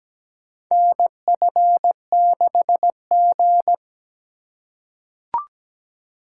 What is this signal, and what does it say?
Morse code, heard as a keyed tone, about 17 wpm, 700 Hz. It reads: NF6G
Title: USA Call Signs - 4 Characters - 3x Character Spacing - 17wpm